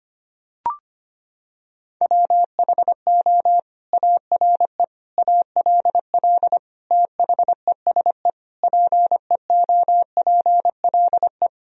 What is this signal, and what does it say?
Morse code, heard as a keyed tone, about 25 wpm, 700 Hz. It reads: WHO ARE ALL THESE PEOPLE
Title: Sentences from Top 1000 Words - Review Part 1 - 25wpm